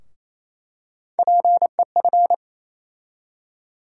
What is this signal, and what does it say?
Morse code, heard as a keyed tone, about 28 wpm, 700 Hz. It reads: PEF
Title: USA Call Sign Postfixes - 3 Letters 28wpm